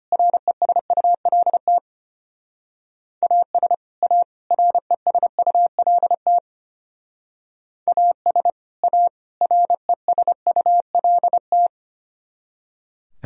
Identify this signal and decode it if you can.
Morse code, heard as a keyed tone, about 31 wpm, 700 Hz. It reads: RESULT AS A RESULT AS A RESULT
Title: Speed-Racing - Sets of 3 Words 25wpm